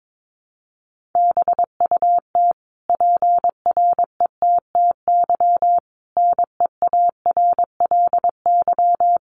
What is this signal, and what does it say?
Morse code, heard as a keyed tone, about 22 wpm, 700 Hz. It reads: BUT PRETTY NEARLY
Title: Sentences from Top 1300 Words 22wpm